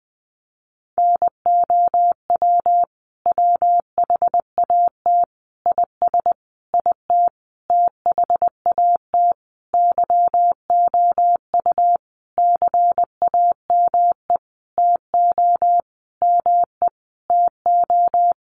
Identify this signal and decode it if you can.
Morse code, heard as a keyed tone, about 20 wpm, 700 Hz. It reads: NOW WHAT IS IT THAT YOU CAME TO ME TO
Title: Sentences from Top 200 Words 20wpm